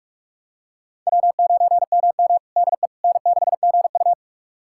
Morse code, heard as a keyed tone, about 44 wpm, 690 Hz. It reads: W9MM DE N6GU
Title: USA Callsign DE Callsign - 4 Characters - 45wpm